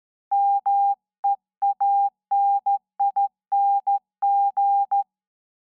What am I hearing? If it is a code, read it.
Morse code, heard as a keyed tone, about 14 wpm, 800 Hz. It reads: MEANING
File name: se / pre2 / Morse Code02-2(Slow)